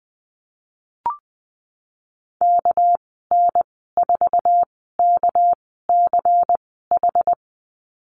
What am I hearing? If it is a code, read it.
Morse code, heard as a keyed tone, about 20 wpm, spaced as for 15 wpm, 700 Hz. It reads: KN4KCH
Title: 213 Lesson 34g - Number 8 - Call Signs 20wpm